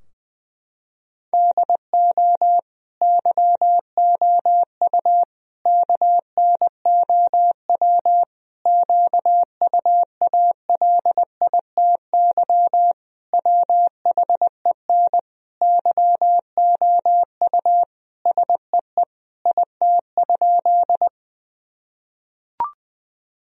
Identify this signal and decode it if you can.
Morse code, heard as a keyed tone, about 21 wpm, 700 Hz. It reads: DO YOU KNOW QUALITY WHEN YOU SEE IT?
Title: Sentences from Top 1000 Words - Review Part 4 - 20wpm